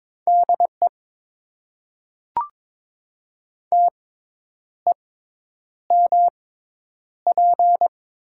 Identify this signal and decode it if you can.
Morse code, heard as a keyed tone, about 22 wpm, spaced as for 9 wpm, 700 Hz. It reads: DE TEMP
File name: Top 90 QSO Elements - 6x Character Spacing - 22wpm